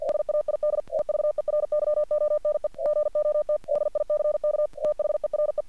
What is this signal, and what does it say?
Morse code, heard as a keyed tone, about 45 wpm, 600 Hz. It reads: SAID EVERYONE GOT DICK THERE
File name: archived / CRSnet